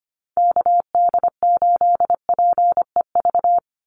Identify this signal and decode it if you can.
Morse code, heard as a keyed tone, about 25 wpm, 700 Hz. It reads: KD8PEV
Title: USA Call Signs 25wpm